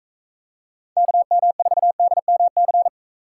Morse code, heard as a keyed tone, about 43 wpm, 700 Hz. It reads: KM4DMC